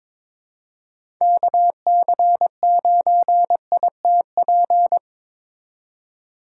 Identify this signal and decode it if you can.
Morse code, heard as a keyed tone, about 22 wpm, 700 Hz. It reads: KC9ITP
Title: USA Call Signs - 6 Characters - 3x Character Spacing - 22wpm